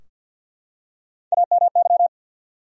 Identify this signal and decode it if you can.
Morse code, heard as a keyed tone, about 53 wpm, 700 Hz. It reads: AMY
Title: USA Common Names - Max 3 Letters - 50wpm